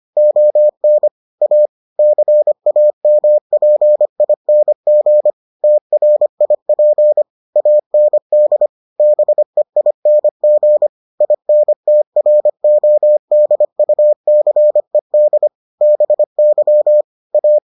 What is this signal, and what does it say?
Morse code, heard as a keyed tone, about 25 wpm, 600 Hz. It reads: ON A CAMPING TRIP AND BEING INTRODUCED BY A